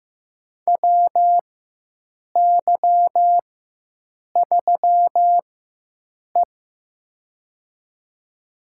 Morse code, heard as a keyed tone, about 15 wpm, spaced as for 7 wpm, 700 Hz. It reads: WY3E
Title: USA Call Signs - 4 Characters - 4x Character Spacing - 15wpm